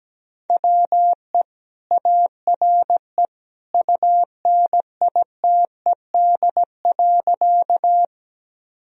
Morse code, heard as a keyed tone, about 17 wpm, 700 Hz. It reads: WE ARE UNITED.
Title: Sentences from Top 1000 Words - Review Part 4 - 17wpm